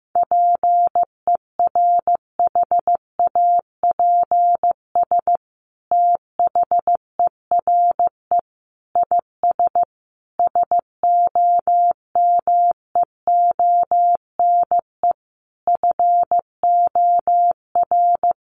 Morse code, heard as a keyed tone, about 15 wpm, 700 Hz. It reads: PERHAPS THERE IS SOMEONE FOR
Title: Sentences from Top 600 Words 15wpm